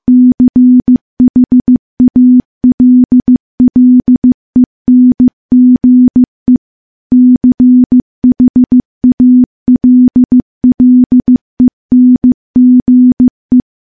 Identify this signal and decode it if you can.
Morse code, heard as a keyed tone, about 15 wpm, 250 Hz. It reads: CHALLENGE CHALLENGE
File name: morse-training-15WPM